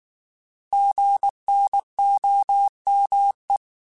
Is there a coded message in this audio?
Yes: Morse code from 0.7 to 3.6 s, heard as a keyed tone, about 19 wpm, 800 Hz. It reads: GNOME